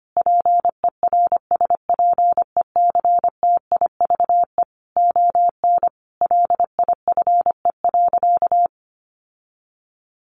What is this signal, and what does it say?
Morse code, heard as a keyed tone, about 25 wpm, 700 Hz. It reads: PERSPECTIVE ON LIFE.
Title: Sentences from Top 1200 Words 25wpm